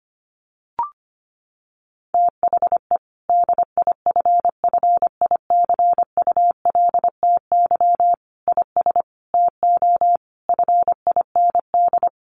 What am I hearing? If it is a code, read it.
Morse code, heard as a keyed tone, about 25 wpm, 700 Hz. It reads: THE DIFFICULTY IS TO FIND
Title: Sentences from Top 1500 Words 25wpm